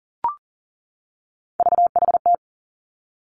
Morse code, heard as a keyed tone, about 41 wpm, 700 Hz. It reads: VHT
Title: Three Challenging Letter-Number Combinations 40wpm